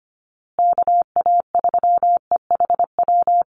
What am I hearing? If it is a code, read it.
Morse code, heard as a keyed tone, about 25 wpm, 700 Hz. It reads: KA3EHW
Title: USA Call Signs Repeated 3x 25wpm